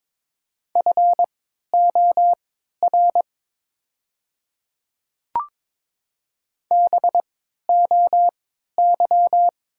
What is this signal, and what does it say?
Morse code, heard as a keyed tone, about 22 wpm, spaced as for 13 wpm, 700 Hz. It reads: FOR BOY
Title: 3x Character Spacing - Top 500 Words - 22wpm